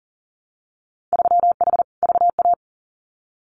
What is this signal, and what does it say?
Morse code, heard as a keyed tone, about 40 wpm, 700 Hz. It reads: 3H VA
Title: Winter Field Day Exchanges 40wpm